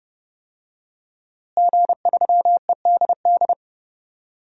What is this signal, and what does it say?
Morse code, heard as a keyed tone, about 30 wpm, 700 Hz. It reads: G3EDD
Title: DXCC-100 Call Signs - 5 Characters - Repeated 3x - 30wpm